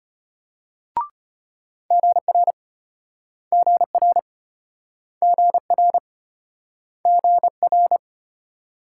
Morse code, heard as a keyed tone, about 33 wpm, 700 Hz. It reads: GR GR GR GR